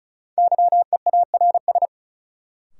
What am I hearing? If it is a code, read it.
Morse code, heard as a keyed tone, about 36 wpm, 700 Hz. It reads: YEARS
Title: Sentences from Top 200 Words - New Word Drill - 35wpm